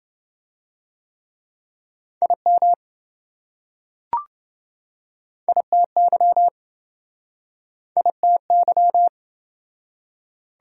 Morse code, heard as a keyed tone, about 26 wpm, 700 Hz. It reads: IM ITY ITY